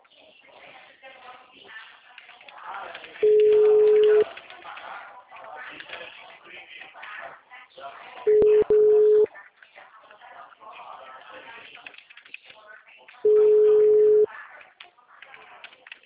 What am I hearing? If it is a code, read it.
Ringback tone (Europe, 425 Hz), with speech.